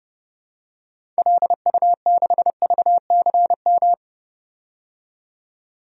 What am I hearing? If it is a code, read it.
Morse code, heard as a keyed tone, about 31 wpm, 700 Hz. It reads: LU6VCM
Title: DXCC-100 Call Signs - 6 Characters - 30wpm